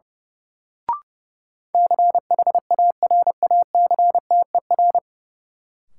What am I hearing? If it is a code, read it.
Morse code, heard as a keyed tone, about 30 wpm, 700 Hz. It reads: CHARACTER
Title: Sentences from Top 1200 Words - New Word Drill Alt - 30wpm